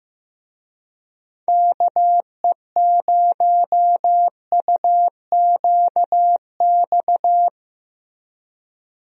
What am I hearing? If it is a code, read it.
Morse code, heard as a keyed tone, about 15 wpm, 700 Hz. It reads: KE0UQX
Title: USA Call Signs - 6 Characters - Repeated 3x - 15wpm